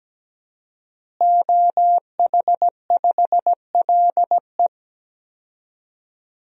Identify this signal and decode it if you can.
Morse code, heard as a keyed tone, about 17 wpm, 700 Hz. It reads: OH5LE
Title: DXCC-100 Call Signs - 5 Characters - Repeated 2x - 17wpm